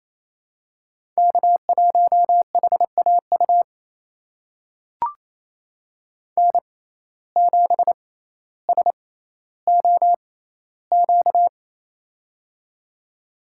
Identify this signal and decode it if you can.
Morse code, heard as a keyed tone, about 28 wpm, spaced as for 14 wpm, 700 Hz. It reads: K1HAU N7SOQ
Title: USA Call Signs - 5 Characters - 6x Character Spacing - 28wpm